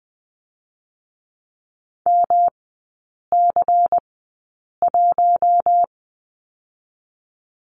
Morse code, heard as a keyed tone, about 20 wpm, 700 Hz. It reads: M C 1
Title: 107 Lesson 18e - Letter m - Three Characters 20wpm